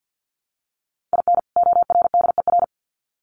Morse code, heard as a keyed tone, about 48 wpm, 700 Hz. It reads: IN ORDER
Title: Sets of 2 Words 50wpm